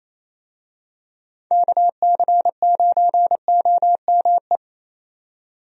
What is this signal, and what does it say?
Morse code, heard as a keyed tone, about 27 wpm, 700 Hz. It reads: KC9OME
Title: USA Call Signs - 6 Characters - 28wpm